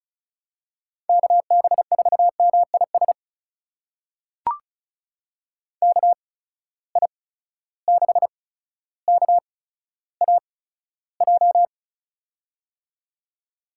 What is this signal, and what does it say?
Morse code, heard as a keyed tone, about 36 wpm, spaced as for 15 wpm, 700 Hz. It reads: KB4MIS KI6KAJ
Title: USA Call Signs - 6 Characters - 8x Character Spacing - 35wpm